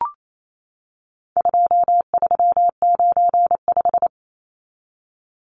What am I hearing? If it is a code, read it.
Morse code, heard as a keyed tone, about 28 wpm, 700 Hz. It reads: 2395
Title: Numbers - 4-Digit 28wpm